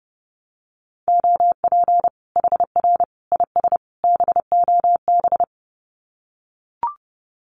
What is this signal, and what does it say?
Morse code, heard as a keyed tone, about 30 wpm, 700 Hz. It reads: OP HR IS BOB
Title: Common QSO Phrases 30wpm